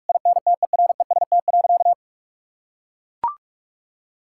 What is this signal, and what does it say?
Morse code, heard as a keyed tone, about 41 wpm, 700 Hz. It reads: INTEREST.
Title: Sentences from Top 1000 Words - Review Part 5 - 45wpm